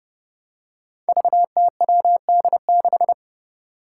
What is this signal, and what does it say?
Morse code, heard as a keyed tone, about 30 wpm, 700 Hz. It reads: VTWD6
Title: Five Challenging Letter-Number Combinations 30wpm